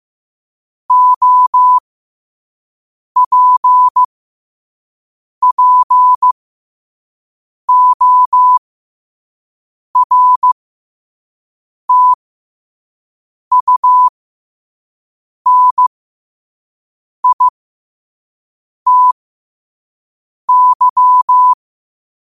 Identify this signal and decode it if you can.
Morse code, heard as a keyed tone, about 15 wpm, spaced as for 5 wpm, 1000 Hz. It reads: OPPORTUNITY
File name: qotdfiles / 5wpm